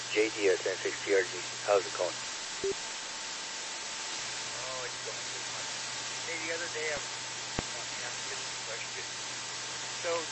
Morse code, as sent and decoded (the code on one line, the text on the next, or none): none